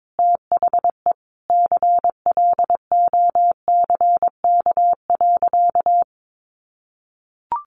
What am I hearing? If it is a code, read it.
Morse code, heard as a keyed tone, about 22 wpm, 700 Hz. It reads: THE CLOCK.
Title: Easy Phrases 22wpm